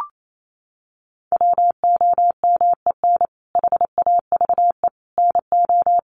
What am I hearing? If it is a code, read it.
Morse code, heard as a keyed tone, about 28 wpm, 700 Hz. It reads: WOMEN HAVE NO
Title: Sentences from Top 1400 Words 28wpm